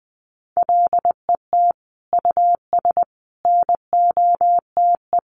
Morse code, heard as a keyed tone, about 20 wpm, 700 Hz. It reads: LET US NOTE
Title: Sentences from Top 1500 Words 20wpm